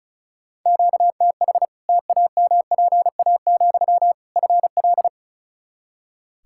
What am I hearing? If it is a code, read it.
Morse code, heard as a keyed tone, about 34 wpm, 700 Hz. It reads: QTH TAMPA, FL
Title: Practice QSO - Exchange 1 - Beginning - 35wpm